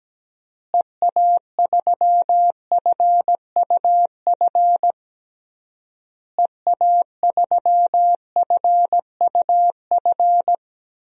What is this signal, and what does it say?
Morse code, heard as a keyed tone, about 17 wpm, 700 Hz. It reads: EA3FUF EA3FUF
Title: DXCC-100 Call Signs - 6 Characters - Repeated 3x - 17wpm